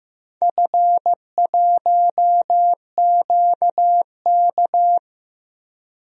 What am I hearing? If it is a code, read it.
Morse code, heard as a keyed tone, about 15 wpm, 700 Hz. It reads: F1QK
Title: DXCC-100 Call Signs - 4 Characters - Repeated 2x - 15wpm